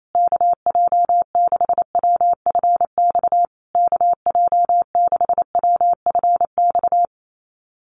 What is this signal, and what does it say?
Morse code, heard as a keyed tone, about 28 wpm, 700 Hz. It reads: KJ6WFX KJ6WFX
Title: USA Call Signs Repeated 3x 28wpm